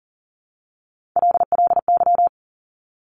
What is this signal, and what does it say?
Morse code, heard as a keyed tone, about 42 wpm, 700 Hz. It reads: LLY
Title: Top Three Letter Combinations in Words 40wpm